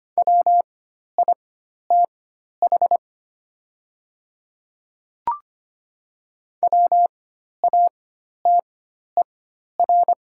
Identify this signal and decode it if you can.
Morse code, heard as a keyed tone, about 25 wpm, spaced as for 12 wpm, 690 Hz. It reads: WITH WATER
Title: 4x Character Spacing - Top 200 Words - 25wpm